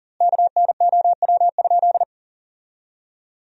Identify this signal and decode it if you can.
Morse code, heard as a keyed tone, about 40 wpm, 700 Hz. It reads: KNOW?
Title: 3x Word-Spacing - Sentences from Top 100 Words - 40wpm